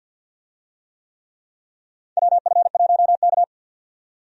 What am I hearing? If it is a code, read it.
Morse code, heard as a keyed tone, about 52 wpm, 700 Hz. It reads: WF1K
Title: USA Call Signs - 4 Characters - 50wpm